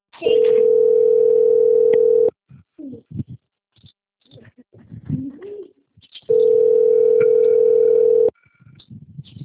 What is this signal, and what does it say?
Ringback tone (North America), 440 + 480 Hz.
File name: out- - - - - .17945